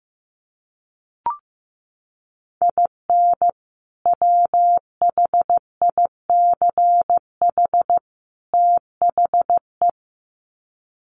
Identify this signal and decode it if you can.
Morse code, heard as a keyed tone, about 15 wpm, 700 Hz. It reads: IN WHICH THE
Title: Sets of 3 Words 15wpm